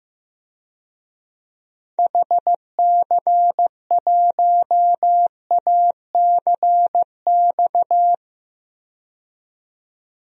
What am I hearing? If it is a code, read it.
Morse code, heard as a keyed tone, about 15 wpm, 700 Hz. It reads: HC1ACX
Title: DXCC-100 Call Signs - 6 Characters - 15wpm